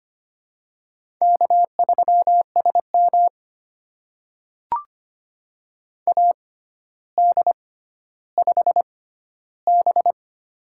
Morse code, heard as a keyed tone, about 25 wpm, spaced as for 13 wpm, 700 Hz. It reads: K3SM AD5B
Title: USA Call Signs - 4 Characters - 6x Character Spacing - 25wpm